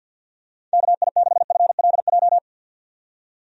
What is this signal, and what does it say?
Morse code, heard as a keyed tone, about 50 wpm, 700 Hz. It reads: KI6FLJ
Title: USA Call Signs - 6 Characters - 50wpm